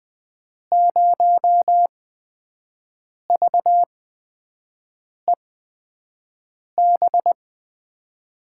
Morse code, heard as a keyed tone, about 20 wpm, spaced as for 5 wpm, 700 Hz. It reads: 0VEB
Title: USA Call Signs - 6 Characters - 8x Character Spacing - 20wpm